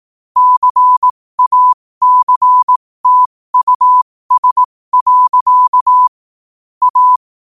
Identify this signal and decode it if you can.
Morse code, heard as a keyed tone, about 18 wpm, spaced as for 15 wpm, 1000 Hz. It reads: CACTUS. A
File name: qotdfiles / 15wpm